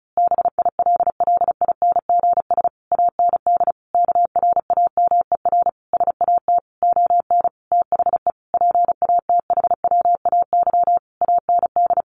Morse code, heard as a keyed tone, about 34 wpm, 700 Hz. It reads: BILLINGS AND KRAMER SAT ON THE PATHWAY AND
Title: A Sound Of Thunder 35wpm